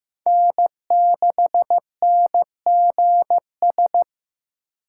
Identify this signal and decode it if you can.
Morse code, heard as a keyed tone, about 15 wpm, 700 Hz. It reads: N6NGS
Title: USA Call Signs - 5 Characters - 3x Character Spacing - 15wpm